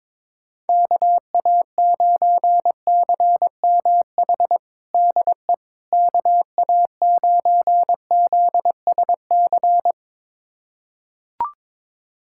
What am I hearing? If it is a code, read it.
Morse code, heard as a keyed tone, about 22 wpm, 700 Hz. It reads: KA9CMH DE KA9ZSC
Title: USA Callsign DE Callsign 22wpm